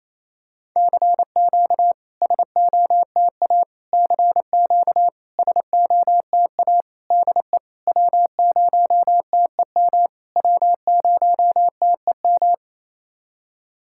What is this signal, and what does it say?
Morse code, heard as a keyed tone, about 27 wpm, 690 Hz. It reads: CQ SOTA CQ SOTA DE W0TEM W0TEM
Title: SOTA QSO Practice 28wpm